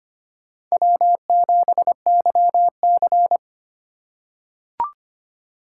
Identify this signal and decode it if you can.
Morse code, heard as a keyed tone, about 25 wpm, 700 Hz. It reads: W7YC
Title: USA Call Signs - 2x Character Spacing - 25wpm